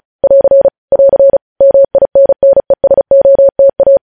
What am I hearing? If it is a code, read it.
Morse code, heard as a keyed tone, about 36 wpm, 550 Hz. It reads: <AR> <AR> MINNESOTA